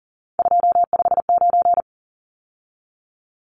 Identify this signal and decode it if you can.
Morse code, heard as a keyed tone, about 40 wpm, 700 Hz. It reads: 259
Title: Numbers - 3-Digit 40wpm